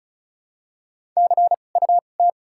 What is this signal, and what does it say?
Morse code, heard as a keyed tone, about 34 wpm, spaced as for 25 wpm, 700 Hz. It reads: CUT
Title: 2x Character Spacing - Top 500 Words - 35wpm